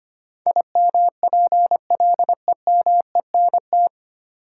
Morse code, heard as a keyed tone, about 25 wpm, 700 Hz. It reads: IMPLEMENT
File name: Nine Letter Words - 25wpm